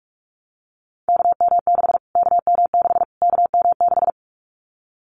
Morse code, heard as a keyed tone, about 44 wpm, 700 Hz. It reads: KM6 KM6 KM6
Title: Callsign Prefixes for Top 25 DXCC - Repeated 3x - 45wpm